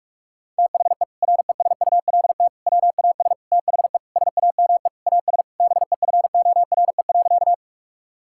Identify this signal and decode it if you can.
Morse code, heard as a keyed tone, about 46 wpm, 700 Hz. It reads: THE RESULT WAS THE SAME AS BEFORE.
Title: Sentences from Top 1000 Words - Review Part 5 - 45wpm